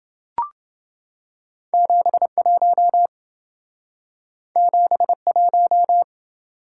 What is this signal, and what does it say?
Morse code, heard as a keyed tone, about 28 wpm, 700 Hz. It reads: 71 71